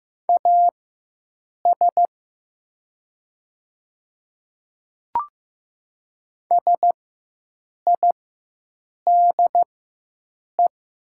Morse code, heard as a keyed tone, about 15 wpm, spaced as for 7 wpm, 700 Hz. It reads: AS SIDE
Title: 4x Character Spacing - Top 300 Words - 15wpm